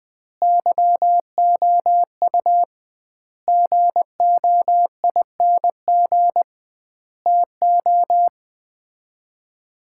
Morse code, heard as a keyed tone, about 20 wpm, 700 Hz. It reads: YOU GOING TO
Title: 2x Word-Spacing - Sets of 5 Words - 20wpm